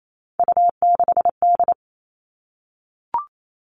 Morse code, heard as a keyed tone, about 28 wpm, 700 Hz. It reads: U6D
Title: All Cut Numbers - 3-Digit 28wpm